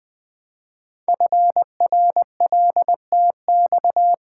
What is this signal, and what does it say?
Morse code, heard as a keyed tone, about 20 wpm, 700 Hz. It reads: FRLTX
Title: Code Groups - 5 Letters - 20wpm